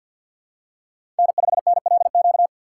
Morse code, heard as a keyed tone, about 53 wpm, 700 Hz. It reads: N5NLQ
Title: USA Call Signs - 5 Characters - 4x Character Spacing - 50wpm